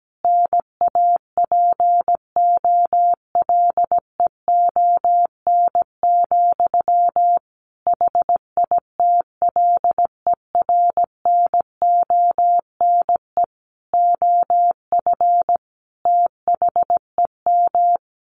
Morse code, heard as a keyed tone, about 17 wpm, 700 Hz. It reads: NAPOLEON, HITLER NONE OF THEM
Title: A Sound Of Thunder 17wpm